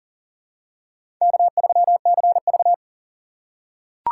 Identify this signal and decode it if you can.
Morse code, heard as a keyed tone, about 40 wpm, 700 Hz. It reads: K3CV
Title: USA Call Signs - 4 Characters - 6x Character Spacing - 40wpm